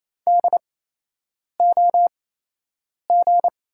Morse code, heard as a keyed tone, about 28 wpm, spaced as for 8 wpm, 700 Hz. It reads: DOG
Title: 8x Character Spacing - Top 500 Words - 28wpm